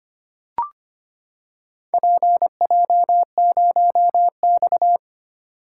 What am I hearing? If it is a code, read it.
Morse code, heard as a keyed tone, about 25 wpm, 700 Hz. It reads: PJ0X
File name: practice / 25wpm / Four Letter-Number Combinations - 25wpm